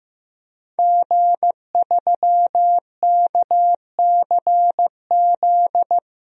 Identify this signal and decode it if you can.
Morse code, heard as a keyed tone, about 15 wpm, 700 Hz. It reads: G3KCZ
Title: DXCC-100 Call Signs - 5 Characters - Repeated 2x - 15wpm